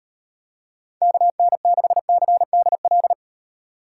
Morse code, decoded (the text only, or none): KN6CDL